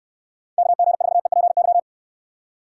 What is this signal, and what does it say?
Morse code, heard as a keyed tone, about 65 wpm, 700 Hz. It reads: KG4EPY